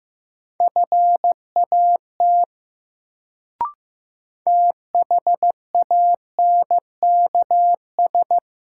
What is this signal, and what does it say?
Morse code, heard as a keyed tone, about 15 wpm, 700 Hz. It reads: FAT THANKS
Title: Sentences from Top 600 Words - New Word Drill - 15wpm